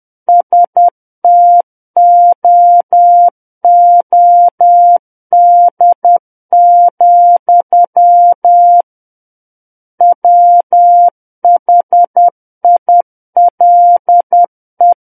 Morse code, heard as a keyed tone, about 10 wpm, 700 Hz. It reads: STOOD, WHILE
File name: Book One / 01-Chapter 1 - 10 WPM